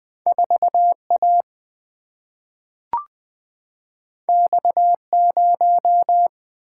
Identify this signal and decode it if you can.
Morse code, heard as a keyed tone, about 20 wpm, 700 Hz. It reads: 4 A X 0
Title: Single-Letter Single-Number 20wpm